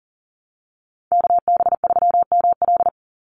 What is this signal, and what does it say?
Morse code, heard as a keyed tone, about 40 wpm, 700 Hz. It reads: KB3ML
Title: Call Signs Repeated 2x - 40wpm